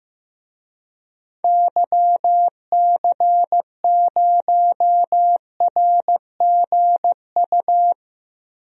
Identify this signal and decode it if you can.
Morse code, heard as a keyed tone, about 15 wpm, 700 Hz. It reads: YC0RGU
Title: DXCC-100 Call Signs - 6 Characters - Repeated 2x - 15wpm